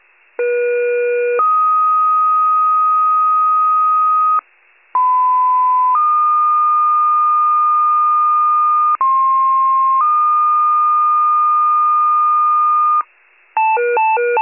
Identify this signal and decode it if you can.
Two-tone page, heard as a one-second tone, then a three-second tone, about 480 then 1250 Hz, about 1010 then 1220 Hz, about 1060 then 1220 Hz. It is heard from 0.4 to 13.0 s.
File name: pager / 94 RESCUE OFC / wcalert